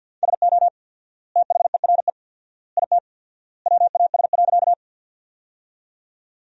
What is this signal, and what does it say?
Morse code, heard as a keyed tone, about 50 wpm, 700 Hz. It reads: SO THERE IT WAS.
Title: 4x Word-Spacing - Sentences from Top 100 Words - 50wpm